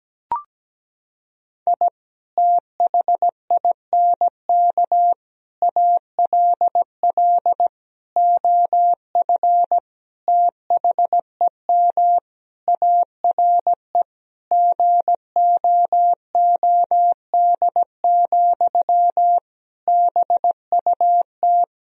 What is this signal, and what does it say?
Morse code, heard as a keyed tone, about 17 wpm, 700 Hz. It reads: I THINK ALL OF THEM ARE GOOD, BUT
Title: Sentences from Top 1000 Words - Review Part 4 - 17wpm